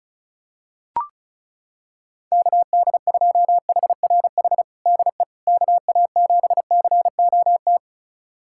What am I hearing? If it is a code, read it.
Morse code, heard as a keyed tone, about 36 wpm, 700 Hz. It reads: KD2HRH DE KA7COT
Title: USA Callsign DE Callsign 35wpm